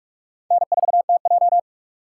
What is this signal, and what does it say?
Morse code, heard as a keyed tone, about 56 wpm, 700 Hz. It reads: N4TJ
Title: Speed Racing - USA Call Signs 45wpm